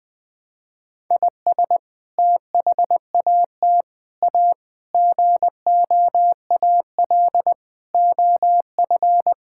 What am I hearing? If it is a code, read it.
Morse code, heard as a keyed tone, about 20 wpm, 700 Hz. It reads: IS THAT A GOAL OF
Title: Sentences from Top 1000 Words - Review Part 5 - 20wpm